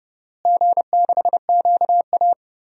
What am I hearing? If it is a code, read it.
Morse code, heard as a keyed tone, about 30 wpm, 700 Hz. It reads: G6QA
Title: DXCC-100 Call Signs - 4 Characters - Repeated 2x - 30wpm